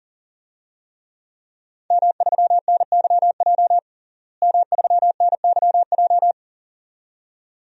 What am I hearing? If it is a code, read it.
Morse code, heard as a keyed tone, about 40 wpm, 700 Hz. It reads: M3NYJ M3NYJ
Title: DXCC-100 Call Signs - 5 Characters - Repeated 2x - 40wpm